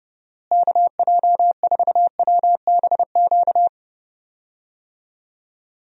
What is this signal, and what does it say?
Morse code, heard as a keyed tone, about 30 wpm, 700 Hz. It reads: KJ4WBQ
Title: USA Call Signs - 6 Characters - Repeated 2x - 30wpm